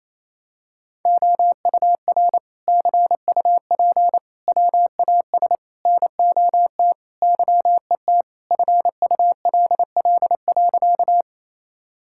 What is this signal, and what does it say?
Morse code, heard as a keyed tone, about 27 wpm, 700 Hz. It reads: OUR CUP WAS NOT YET FULL.
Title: Sentences from Top 1600 Words 28wpm